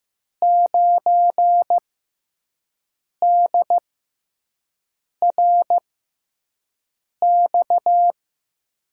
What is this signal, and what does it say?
Morse code, heard as a keyed tone, about 15 wpm, spaced as for 5 wpm, 700 Hz. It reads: 9DRX
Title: USA Call Signs - 6x Character Spacing - 15wpm